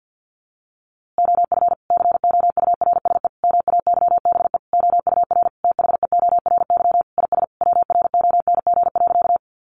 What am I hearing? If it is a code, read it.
Morse code, heard as a keyed tone, about 50 wpm, 700 Hz. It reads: OF COURSE MAYBE OUR THEORY IS WRONG.
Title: A Sound Of Thunder 50wpm